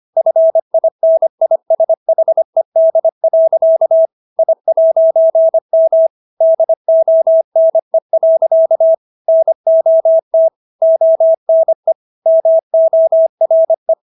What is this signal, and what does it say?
Morse code, heard as a keyed tone, about 25 wpm, 650 Hz. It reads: FINISHED. I'M DONE. NOT ONE MORE